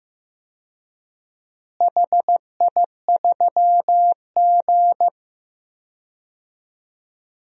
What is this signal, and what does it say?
Morse code, heard as a keyed tone, about 15 wpm, 700 Hz. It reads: HI3G
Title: DXCC-100 Call Signs - 4 Characters - Repeated 2x - 15wpm